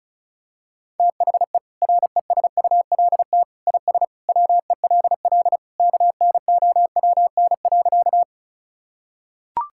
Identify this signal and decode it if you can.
Morse code, heard as a keyed tone, about 36 wpm, 700 Hz. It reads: THE RESULT IS WELL KNOWN.
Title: Sentences from Top 1000 Words - Review Part 5 - 35wpm